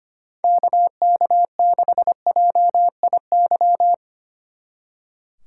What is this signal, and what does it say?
Morse code, heard as a keyed tone, about 32 wpm, 700 Hz. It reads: KK6JIY